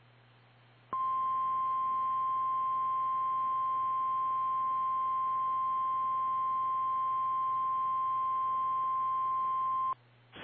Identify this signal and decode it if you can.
Emergency Alert System (SAME) tones (the 1050 Hz attention tone). It is heard from 0.9 to 10.0 s.
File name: eas / KKPRKXPN / KKPRKXPN